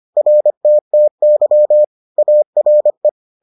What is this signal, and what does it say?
Morse code, heard as a keyed tone, about 25 wpm, 600 Hz. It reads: RTTY ARE